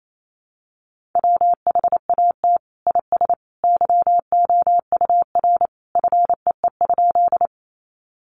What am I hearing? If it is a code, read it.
Morse code, heard as a keyed tone, about 28 wpm, 700 Hz. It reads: WHAT IS YOUR FEE?
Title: Sentences from Top 1500 Words 28wpm